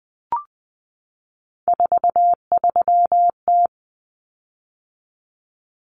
Morse code, heard as a keyed tone, about 20 wpm, 700 Hz. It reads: 43T
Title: Common Cut Numbers - 3-Digit 20wpm